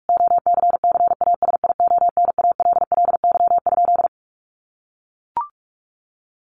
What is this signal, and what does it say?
Morse code, heard as a keyed tone, about 44 wpm, 700 Hz. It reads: OCCASIONALLY?
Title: Sentences from Top 1300 Words 45wpm